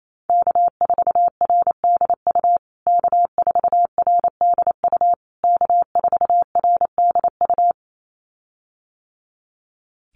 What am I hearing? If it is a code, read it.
Morse code, heard as a keyed tone, about 28 wpm, 700 Hz. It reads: K4RDU K4RDU K4RDU
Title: USA Call Signs Repeated 3x 28wpm